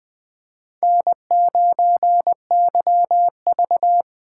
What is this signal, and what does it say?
Morse code, heard as a keyed tone, about 20 wpm, 700 Hz. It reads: N9YV
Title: USA Call Signs - 4 Characters - 4x Character Spacing - 20wpm